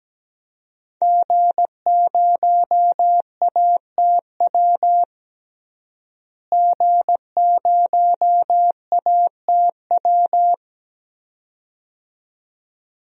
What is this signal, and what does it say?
Morse code, heard as a keyed tone, about 17 wpm, 700 Hz. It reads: G0ATW G0ATW
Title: DXCC-100 Call Signs - 5 Characters - Repeated 2x - 17wpm